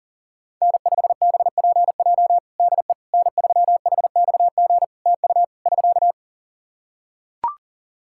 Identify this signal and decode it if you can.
Morse code, heard as a keyed tone, about 40 wpm, 700 Hz. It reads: N5BPJ DE N3HXG TU <SK>
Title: K1USN SST QSO Practice 40wpm